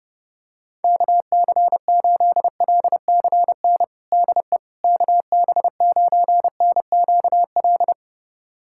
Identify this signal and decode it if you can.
Morse code, heard as a keyed tone, about 30 wpm, 700 Hz. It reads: KC8LCN DE KB9NQL
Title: USA Callsign DE Callsign - 6 Characters - 30wpm